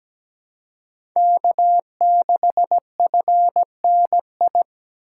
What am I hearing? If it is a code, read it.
Morse code, heard as a keyed tone, about 17 wpm, 700 Hz. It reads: K6FNI
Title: USA Call Signs - 5 Characters - 3x Character Spacing - 17wpm